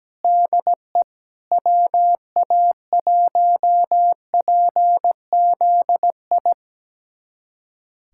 Morse code, heard as a keyed tone, about 17 wpm, 700 Hz. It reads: DE WA1PZI
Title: USA Callsign DE Callsign 17wpm